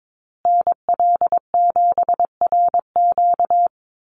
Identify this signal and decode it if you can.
Morse code, heard as a keyed tone, about 22 wpm, 700 Hz. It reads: NL7RQ
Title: USA Call Signs Repeated 3x 22wpm